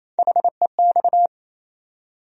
Morse code, heard as a keyed tone, about 28 wpm, 700 Hz. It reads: HEX
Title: USA Call Sign Postfixes - 3 Letters 28wpm